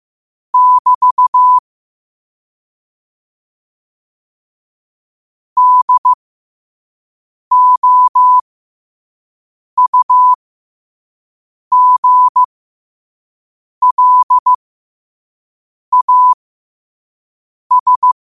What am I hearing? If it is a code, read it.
Morse code, heard as a keyed tone, about 15 wpm, spaced as for 5 wpm, 1000 Hz. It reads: <BT> DOUGLAS
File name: qotdfiles / 5wpm